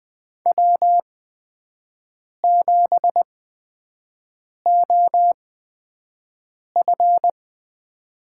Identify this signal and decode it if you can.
Morse code, heard as a keyed tone, about 20 wpm, spaced as for 5 wpm, 700 Hz. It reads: W7OF
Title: USA Call Signs - 4 Characters - 8x Character Spacing - 20wpm